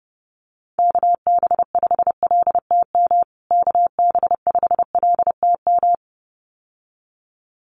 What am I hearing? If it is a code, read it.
Morse code, heard as a keyed tone, about 30 wpm, 700 Hz. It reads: KB5LTM KB5LTM
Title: USA Call Signs Repeated 2x 30wpm